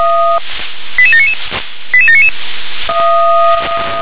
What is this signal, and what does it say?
Five-tone selective call (ZVEI), heard as quick runs of five tones, tones of about 70 ms. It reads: 88799 88799